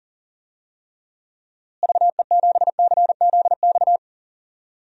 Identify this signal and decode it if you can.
Morse code, heard as a keyed tone, about 40 wpm, 700 Hz. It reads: VE7CZX
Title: DXCC-100 Call Signs - 6 Characters - Repeated 2x - 40wpm